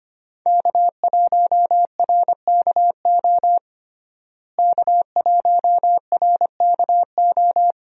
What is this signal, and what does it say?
Morse code, heard as a keyed tone, about 25 wpm, 700 Hz. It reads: K1RKO K1RKO
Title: USA Call Signs - 5 Characters - Repeated 2x - 25wpm